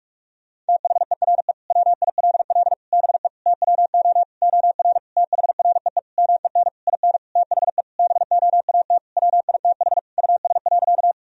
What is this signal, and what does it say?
Morse code, heard as a keyed tone, about 46 wpm, 700 Hz. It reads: THERE WILL BE TWO OR THREE MEN IN THE BOAT WITH US.
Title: Sentences from Top 1000 Words - Review Part 3 - 45wpm